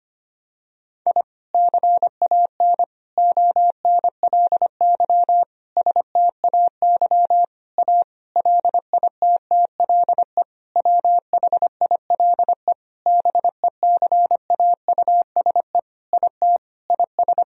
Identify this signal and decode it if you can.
Morse code, heard as a keyed tone, about 25 wpm, 700 Hz. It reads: I CAN ONLY STAY A LITTLE WHILE BECAUSE IT IS
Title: Sentences from Top 1000 Words - Review Part 3 - 25wpm